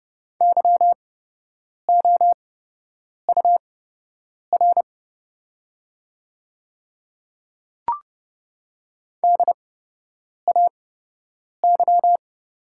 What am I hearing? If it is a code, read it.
Morse code, heard as a keyed tone, about 30 wpm, spaced as for 8 wpm, 700 Hz. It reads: YOUR DAY
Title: 8x Character Spacing - Top 100 Words - 30wpm